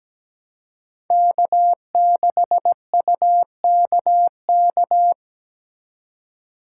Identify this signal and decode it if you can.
Morse code, heard as a keyed tone, about 17 wpm, 700 Hz. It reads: K6UKK
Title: USA Call Signs - 5 Characters - 3x Character Spacing - 17wpm